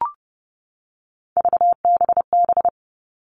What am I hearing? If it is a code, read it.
Morse code, heard as a keyed tone, about 30 wpm, 700 Hz. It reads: VBB
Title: Three Challenging Letter-Number Combinations 30wpm